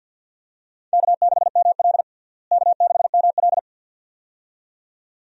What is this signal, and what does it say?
Morse code, heard as a keyed tone, about 52 wpm, 700 Hz. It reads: K6ML K6ML
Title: USA Call Signs - 4 Characters - Repeated 2x - 50wpm